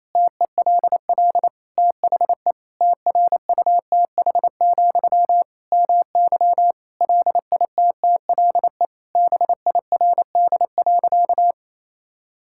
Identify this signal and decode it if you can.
Morse code, heard as a keyed tone, about 28 wpm, 700 Hz. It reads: TELL THE TRUTH, MY LITTLE BIRD.
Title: Sentences from Top 1000 Words - Review Part 6 - 28wpm